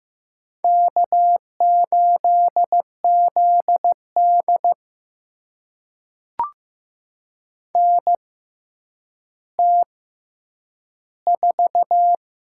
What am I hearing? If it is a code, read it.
Morse code, heard as a keyed tone, about 15 wpm, spaced as for 9 wpm, 700 Hz. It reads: K8ZD NT4
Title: USA Call Signs - 4 Characters - 6x Character Spacing - 15wpm